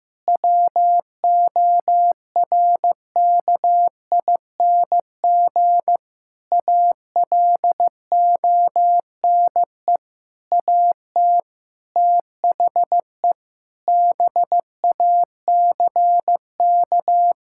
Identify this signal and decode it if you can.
Morse code, heard as a keyed tone, about 16 wpm, 700 Hz. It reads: WORKING ALONE AT THE BACK
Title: Sentences from Top 1000 Words - Review Part 5 - 15wpm